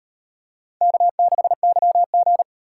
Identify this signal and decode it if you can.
Morse code, heard as a keyed tone, about 34 wpm, 700 Hz. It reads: K6YG